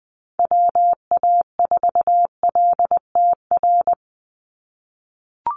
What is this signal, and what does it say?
Morse code, heard as a keyed tone, about 20 wpm, 700 Hz. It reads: WA4LTR
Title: USA Call Signs 20wpm